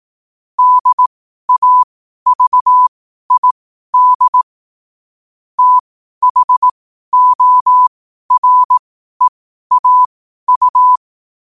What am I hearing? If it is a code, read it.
Morse code, heard as a keyed tone, about 18 wpm, spaced as for 12 wpm, 1000 Hz. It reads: DAVID THOREAU
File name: qotdfiles / 12wpm